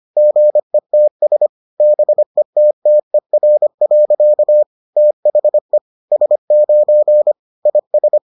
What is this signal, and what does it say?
Morse code, heard as a keyed tone, about 25 wpm, 600 Hz. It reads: GETS BETTER. THE S9 IS